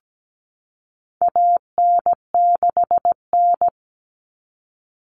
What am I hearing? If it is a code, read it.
Morse code, heard as a keyed tone, about 17 wpm, 700 Hz. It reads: AN6N
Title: Common Cut Numbers - 4-Digit 17wpm